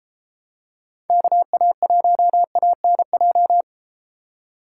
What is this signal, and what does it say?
Morse code, heard as a keyed tone, about 32 wpm, 700 Hz. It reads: KA1ANJ